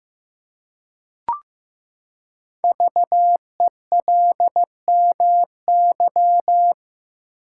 Morse code, heard as a keyed tone, about 15 wpm, 700 Hz. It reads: VELMY
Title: Code Groups - 5 Letters - 15wpm